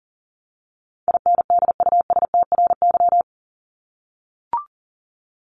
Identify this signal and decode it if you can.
Morse code, heard as a keyed tone, about 37 wpm, 700 Hz. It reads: INDUSTRY
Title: Top 501-750 Words 40wpm